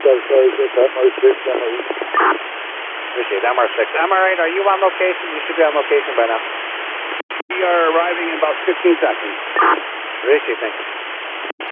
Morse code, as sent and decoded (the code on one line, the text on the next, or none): none